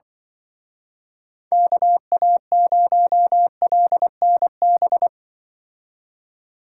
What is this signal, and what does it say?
Morse code, heard as a keyed tone, about 23 wpm, 700 Hz. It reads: KA0LNB
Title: Speed Racing - USA Call Signs 20wpm